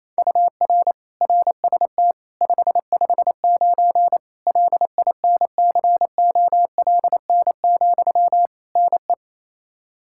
Morse code, heard as a keyed tone, about 28 wpm, 700 Hz. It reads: UR RST 559 LINCOLN, NE
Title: Practice QSO - Exchange 1 - Beginning - 28wpm